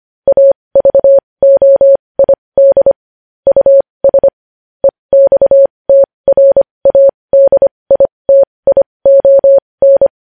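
Morse code, heard as a keyed tone, about 25 wpm, spaced as for 20 wpm, 550 Hz. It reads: AVOID US EXTRADITION
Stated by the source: Morse code practice of news headlines at 20 wpm.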